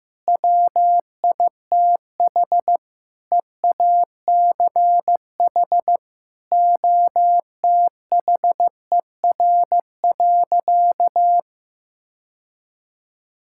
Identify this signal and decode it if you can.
Morse code, heard as a keyed tone, about 15 wpm, 700 Hz. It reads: WITH EACH OTHER.
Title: Sentences from Top 1000 Words - Review Part 6 - 15wpm